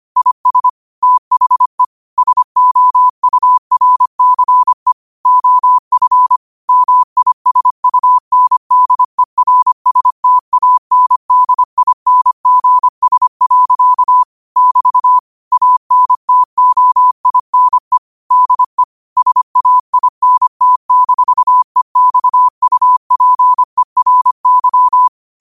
Morse code, heard as a keyed tone, about 25 wpm, 1000 Hz. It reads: IS THE SOURCE OF MISUNDERSTANDINGS. <BT> ANTOINE DE SAINT-EXUPERY
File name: qotdfiles / 25wpm